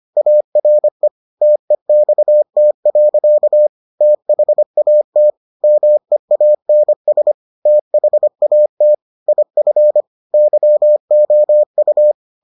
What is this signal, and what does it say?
Morse code, heard as a keyed tone, about 25 wpm, 600 Hz. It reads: ARE TEXT. THAT MEANS THAT IF YOU